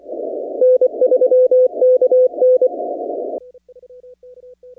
Morse code, heard as a keyed tone, about 24 wpm, 1500 Hz, with noise. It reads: N3KN
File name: contests / 2019-cw-wpx-kt5j-k5pi- -2330 0-15